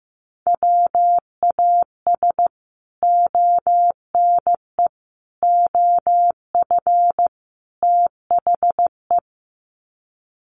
Morse code, heard as a keyed tone, about 15 wpm, 700 Hz. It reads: WAS ONE OF THE
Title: Sets of 4 Words 15wpm